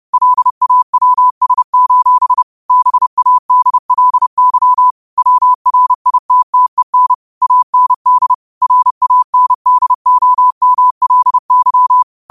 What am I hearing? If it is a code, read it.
Morse code, heard as a keyed tone, about 30 wpm, 1000 Hz. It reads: LAWS: BADLY WRITTEN AND RANDOMLY
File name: qotdfiles / 30wpm